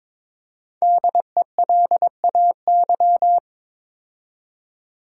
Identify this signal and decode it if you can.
Morse code, heard as a keyed tone, about 22 wpm, 700 Hz. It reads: DELAY
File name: 5 Letter Words - First 3 Letters Given - 22wpm